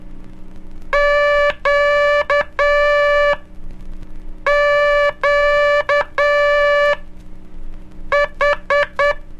Morse code, heard as a keyed tone, about 8 wpm, spaced as for 5 wpm, 580 Hz. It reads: QQH
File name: cursos / Curso CW mp3 / CW Aula - 04